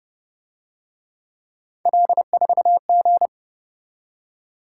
Morse code, heard as a keyed tone, about 30 wpm, 700 Hz. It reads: L4G
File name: Speed-Racing - Three Letter-Number Combinations - 25wpm